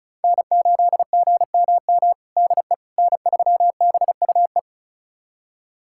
Morse code, heard as a keyed tone, about 36 wpm, 700 Hz. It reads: N8GMM DE N3BUE
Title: USA Callsign DE Callsign - 5 Characters - 35wpm